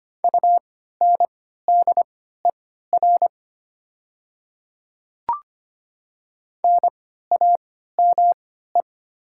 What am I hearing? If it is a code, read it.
Morse code, heard as a keyed tone, about 25 wpm, spaced as for 14 wpm, 700 Hz. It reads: UNDER NAME
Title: 3x Character Spacing - Top 300 Words - 25wpm